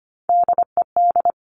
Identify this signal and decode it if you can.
Morse code, heard as a keyed tone, about 25 wpm, 700 Hz. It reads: DED
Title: Top Three Letter Combinations in Words 25wpm